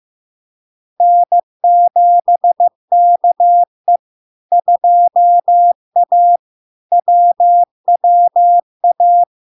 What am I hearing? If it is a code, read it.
Morse code, heard as a keyed tone, about 15 wpm, 700 Hz. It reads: N7KE 2A WWA
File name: LARK-FD-2022-Log-in-CW-15wpm-700-Hz